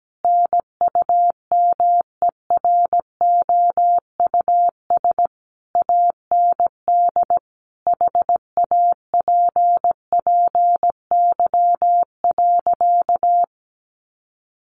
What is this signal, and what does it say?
Morse code, heard as a keyed tone, about 17 wpm, 700 Hz. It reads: NUMEROUS AND HAPPY.
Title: Sentences from Top 1400 Words 17wpm